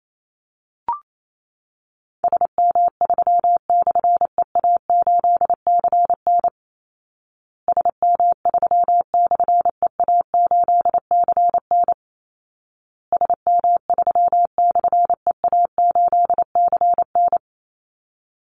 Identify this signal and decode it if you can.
Morse code, heard as a keyed tone, about 28 wpm, 700 Hz. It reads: SM3/EA8CN SM3/EA8CN SM3/EA8CN
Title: Call Signs Repeated 2x - 28wpm